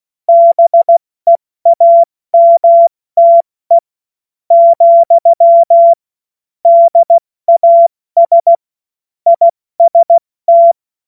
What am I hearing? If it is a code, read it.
Morse code, heard as a keyed tone, about 16 wpm, spaced as for 13 wpm, 680 Hz. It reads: BEAMTE, DAS IST